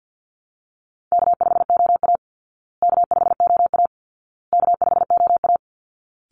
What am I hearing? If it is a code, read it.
Morse code, heard as a keyed tone, about 50 wpm, 700 Hz. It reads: K5OA K5OA K5OA
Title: Call Signs Repeated 2x - 50wpm